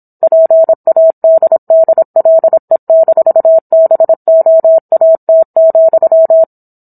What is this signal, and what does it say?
Morse code, heard as a keyed tone, about 27 wpm, 650 Hz. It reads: PADDLE-BOAT,
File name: Book One / 17-Chapter 17 - 26 WPM